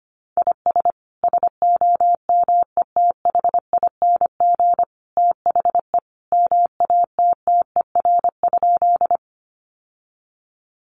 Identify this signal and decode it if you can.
Morse code, heard as a keyed tone, about 25 wpm, 700 Hz. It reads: IS SOMETHING THE MATTER?
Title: Sentences from Top 700 Words 25wpm